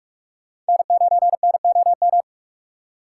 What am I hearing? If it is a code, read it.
Morse code, heard as a keyed tone, about 55 wpm, 700 Hz. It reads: N9NOM
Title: Speed Racing - USA 1x3 Call Signs 45wpm